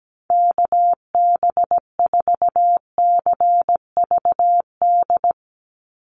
Morse code, heard as a keyed tone, about 17 wpm, 700 Hz. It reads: KB4CVD
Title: USA Call Signs 17wpm